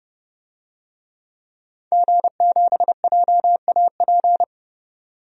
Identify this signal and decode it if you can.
Morse code, heard as a keyed tone, about 30 wpm, 690 Hz. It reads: G7JAP
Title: DXCC-100 Call Signs - 5 Characters - 30wpm